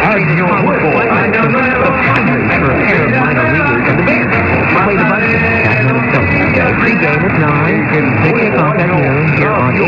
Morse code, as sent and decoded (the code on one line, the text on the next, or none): none